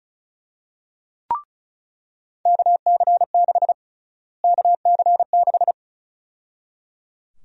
Morse code, heard as a keyed tone, about 34 wpm, 700 Hz. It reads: KC6 KC6
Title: USA Call Sign Prefixes - Repeated 2x - 35wpm